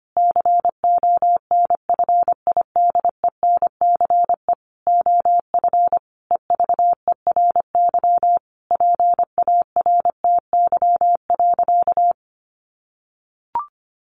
Morse code, heard as a keyed tone, about 25 wpm, 700 Hz. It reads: CONFIDENCE OF EVERY PARTY.
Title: Sentences from Top 1500 Words 25wpm